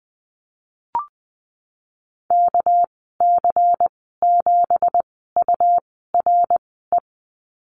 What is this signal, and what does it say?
Morse code, heard as a keyed tone, about 20 wpm, spaced as for 15 wpm, 700 Hz. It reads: KC7URE
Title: 213 Lesson 34g - Number 8 - Call Signs 20wpm